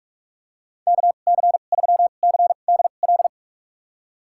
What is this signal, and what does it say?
Morse code, heard as a keyed tone, about 46 wpm, spaced as for 32 wpm, 700 Hz. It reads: KC3CDL
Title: USA Call Signs - 6 Characters - 2x Character Spacing - 45wpm